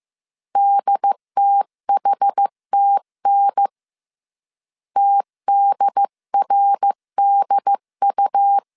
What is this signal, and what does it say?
Morse code, heard as a keyed tone, about 15 wpm, 800 Hz. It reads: DTHTN TDRDU
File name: CW / DET / 1 3 1500C S C 14wpm 2458